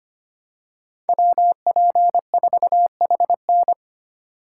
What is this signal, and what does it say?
Morse code, heard as a keyed tone, about 25 wpm, 700 Hz. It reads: WP4HN
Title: USA Call Signs - 5 Characters - 6x Character Spacing - 25wpm